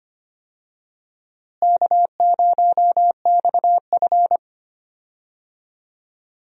Morse code, heard as a keyed tone, about 25 wpm, 700 Hz. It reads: K0XF
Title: USA Call Signs - 4 Characters - Repeated 2x - 25wpm